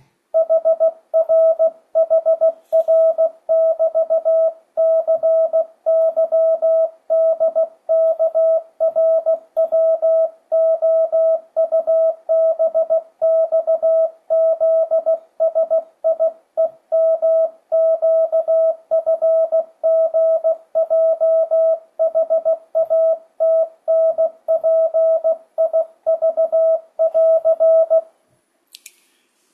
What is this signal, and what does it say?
Morse code, heard as a keyed tone, about 16 wpm, 660 Hz. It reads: HR HR <BT> CYDKRWOUBXZSIEMQFGJHATNPIV <AR>
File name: 2025cw / speedup / 16wpm